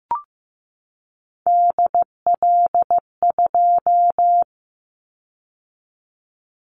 Morse code, heard as a keyed tone, about 15 wpm, 700 Hz. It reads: DL2
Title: Callsign Prefixes for Top 50 DXCC - 15wpm